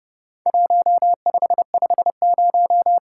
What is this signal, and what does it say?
Morse code, heard as a keyed tone, about 30 wpm, 700 Hz. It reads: 1550
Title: Worked All Europe DX Contest - QTC Exchange - 30wpm